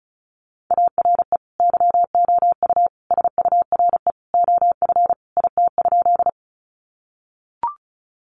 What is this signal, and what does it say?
Morse code, heard as a keyed tone, about 34 wpm, 700 Hz. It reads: ARE YOU SURE OF IT?
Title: Sentences from Top 300 Words 35wpm